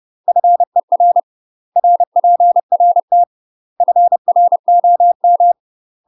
Morse code, heard as a keyed tone, about 30 wpm, 700 Hz. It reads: FER RPRT FROM
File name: UK MASTER QSOs TRAINING / 30WPM / UK MASTER QSOs 0033